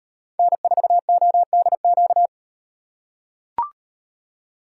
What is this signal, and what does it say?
Morse code, heard as a keyed tone, about 38 wpm, 700 Hz. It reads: N4ODQ